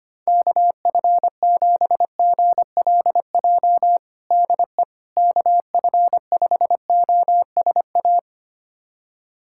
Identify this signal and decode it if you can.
Morse code, heard as a keyed tone, about 25 wpm, 700 Hz. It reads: KF7GLJ DE KF5OSA
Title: USA Callsign DE Callsign - 6 Characters - 25wpm